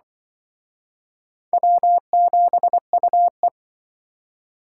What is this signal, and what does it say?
Morse code, heard as a keyed tone, about 25 wpm, 700 Hz. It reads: W7UE